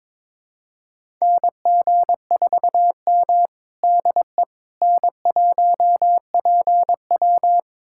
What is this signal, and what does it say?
Morse code, heard as a keyed tone, about 22 wpm, 700 Hz. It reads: NG4M DE N1PW
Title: USA Callsign DE Callsign - 4 Characters - 22wpm